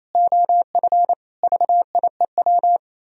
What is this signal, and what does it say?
Morse code, heard as a keyed tone, about 36 wpm, 700 Hz. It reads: OF VIEW